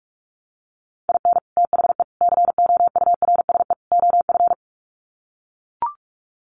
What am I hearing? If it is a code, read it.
Morse code, heard as a keyed tone, about 44 wpm, 700 Hz. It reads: IN THE COURSE OF
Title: Sets of 4 Words 45wpm